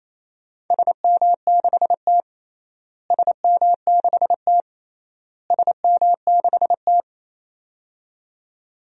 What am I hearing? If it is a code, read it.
Morse code, heard as a keyed tone, about 27 wpm, 700 Hz. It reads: SM6T SM6T SM6T
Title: DXCC-100 Call Signs - 4 Characters - Repeated 3x - 28wpm